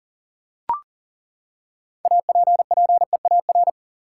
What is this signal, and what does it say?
Morse code, heard as a keyed tone, about 40 wpm, 700 Hz. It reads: APPEAR
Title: Six Letter Words - 40wpm